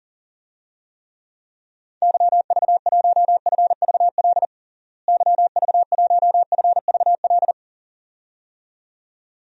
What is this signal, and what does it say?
Morse code, heard as a keyed tone, about 40 wpm, 700 Hz. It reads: YV1FVL YV1FVL
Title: DXCC-100 Call Signs - 6 Characters - Repeated 2x - 40wpm